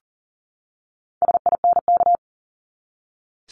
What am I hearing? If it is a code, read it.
Morse code, heard as a keyed tone, about 40 wpm, 700 Hz. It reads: SINK
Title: Top 1751-2000 Words 40wpm